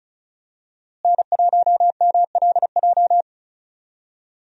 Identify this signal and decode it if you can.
Morse code, heard as a keyed tone, about 36 wpm, 700 Hz. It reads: N1MLJ
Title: USA Call Signs - 5 Characters - Repeated 3x - 35wpm